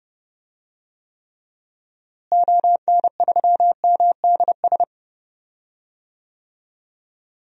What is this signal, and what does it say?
Morse code, heard as a keyed tone, about 30 wpm, 700 Hz. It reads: ON3MDS
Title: DXCC-100 Call Signs - 6 Characters - Repeated 2x - 30wpm